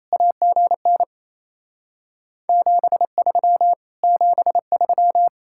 Morse code, heard as a keyed tone, about 37 wpm, 700 Hz. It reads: AGN 73 73